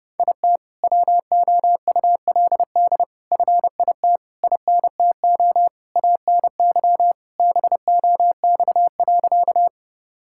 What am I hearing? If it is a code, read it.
Morse code, heard as a keyed tone, about 32 wpm, 700 Hz. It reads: IT WOULD FIT INTO ANY BOX.
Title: Sentences from Top 1000 Words - Review Part 5 - 30wpm